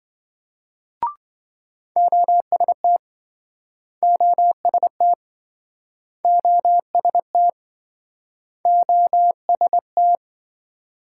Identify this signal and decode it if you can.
Morse code, heard as a keyed tone, about 28 wpm, 700 Hz. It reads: OST OST OST OST